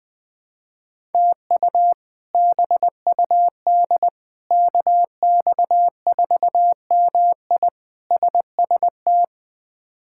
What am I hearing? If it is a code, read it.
Morse code, heard as a keyed tone, about 20 wpm, 700 Hz. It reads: TU BUD KX4MI SST
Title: K1USN SST QSO Practice - 20wpm